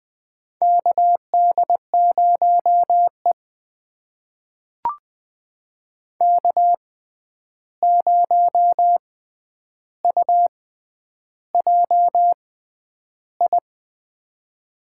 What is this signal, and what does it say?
Morse code, heard as a keyed tone, about 20 wpm, spaced as for 10 wpm, 700 Hz. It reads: KD0E K0UJI
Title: USA Call Signs - 6x Character Spacing - 20wpm